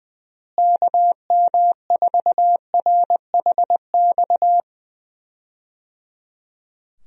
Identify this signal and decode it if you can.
Morse code, heard as a keyed tone, about 20 wpm, 700 Hz. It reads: KM4RHX
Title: USA Call Signs - 6 Characters - Repeated 3x - 20wpm